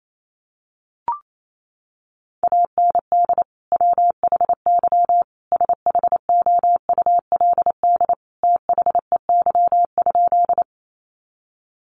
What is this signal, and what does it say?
Morse code, heard as a keyed tone, about 28 wpm, 700 Hz. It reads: AND WHY SHOULD THEY?
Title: Sentences from Top 300 Words 28wpm